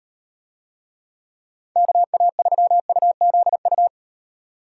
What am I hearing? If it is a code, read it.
Morse code, heard as a keyed tone, about 38 wpm, 700 Hz. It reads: KA3UZU